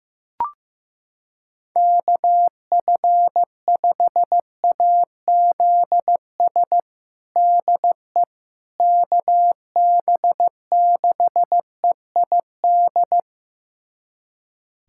Morse code, heard as a keyed tone, about 15 wpm, 700 Hz. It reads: KF5AZS DE KB6EID
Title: USA Callsign DE Callsign - 6 Characters - 15wpm